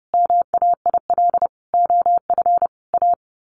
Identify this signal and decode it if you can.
Morse code, heard as a keyed tone, about 30 wpm, 700 Hz. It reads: MAIL OF A
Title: A Sound Of Thunder 30wpm